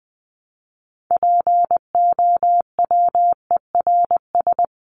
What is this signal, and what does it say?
Morse code, heard as a keyed tone, about 20 wpm, 700 Hz. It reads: POWERS
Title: ES:S Suffix - Top 100 Words with Base Words 20wpm